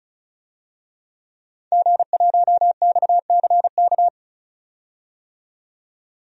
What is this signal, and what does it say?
Morse code, heard as a keyed tone, about 34 wpm, 700 Hz. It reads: G1XCK
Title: DXCC-100 Call Signs - 5 Characters - 35wpm